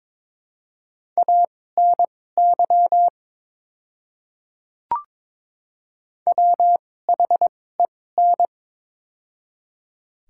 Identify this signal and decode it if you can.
Morse code, heard as a keyed tone, about 22 wpm, spaced as for 16 wpm, 700 Hz. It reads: ANY WHEN
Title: 2x Character Spacing - Top 200 Words - 22wpm